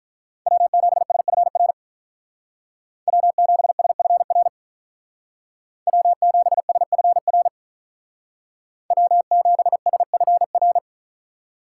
Morse code, heard as a keyed tone, about 46 wpm, 700 Hz. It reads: W7SFR W7SFR W7SFR W7SFR